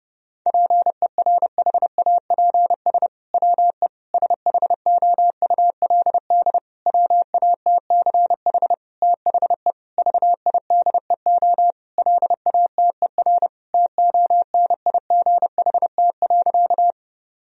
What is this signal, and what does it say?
Morse code, heard as a keyed tone, about 30 wpm, 700 Hz. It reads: PERHAPS WE SHOULD WATCH THE VIDEO LATER TONIGHT.
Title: Sentences from Top 1000 Words - Review Part 4 - 30wpm